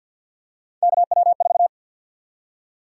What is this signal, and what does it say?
Morse code, heard as a keyed tone, about 51 wpm, 700 Hz. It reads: KW4
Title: USA Call Sign Prefixes - 50wpm